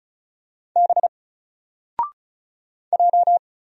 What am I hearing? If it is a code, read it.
Morse code, heard as a keyed tone, about 35 wpm, 700 Hz. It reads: B J
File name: Single-Letter-Number-35wpm